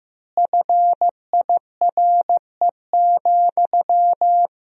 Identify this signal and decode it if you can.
Morse code, heard as a keyed tone, about 15 wpm, 700 Hz. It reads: FIRE,
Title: 3x Word-Spacing - Sentences from Top 400 Words - 15wpm